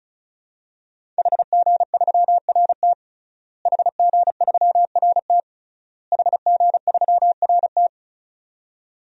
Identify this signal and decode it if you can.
Morse code, heard as a keyed tone, about 36 wpm, 700 Hz. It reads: HG3RT HG3RT HG3RT
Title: DXCC-100 Call Signs - 5 Characters - Repeated 3x - 35wpm